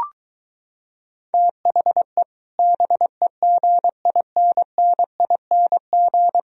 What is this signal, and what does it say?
Morse code, heard as a keyed tone, about 23 wpm, 700 Hz. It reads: THE BEGINNING